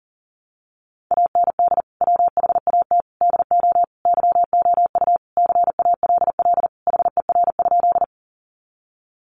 Morse code, heard as a keyed tone, about 40 wpm, 700 Hz. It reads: AND WHAT DO YOU CALL HER?
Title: Sentences from Top 100 Words 40wpm